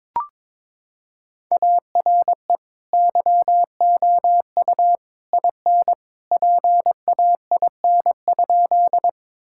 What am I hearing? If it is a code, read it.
Morse code, heard as a keyed tone, about 22 wpm, 700 Hz. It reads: ARE YOU IN PAIN?
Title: Sentences from Top 1600 Words 22wpm